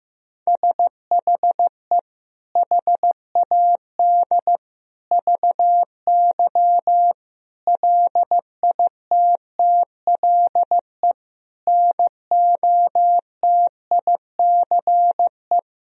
Morse code, heard as a keyed tone, about 15 wpm, 700 Hz. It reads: SHE HAD VY LITTLE NOTICE
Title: Sentences from Top 400 Words and Abbreviations 15wpm